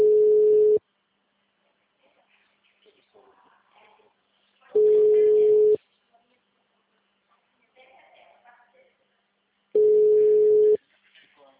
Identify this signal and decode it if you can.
Ringback tone (Europe, 425 Hz), with speech.
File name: out- -2010- - - .28484